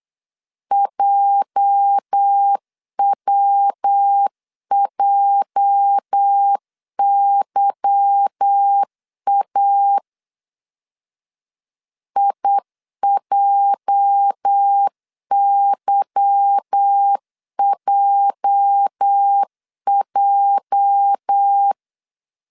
Morse code, heard as a keyed tone, about 8 wpm, 800 Hz. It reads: JWJYA IJYJJ
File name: LEZIONI / 8 YJ45 / 8 YJ 8wpm 3313